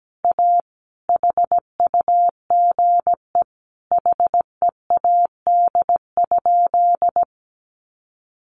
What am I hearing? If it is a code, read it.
Morse code, heard as a keyed tone, about 17 wpm, 700 Hz. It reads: A HUGE HEAD?
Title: Sentences from Top 1100 Words 17wpm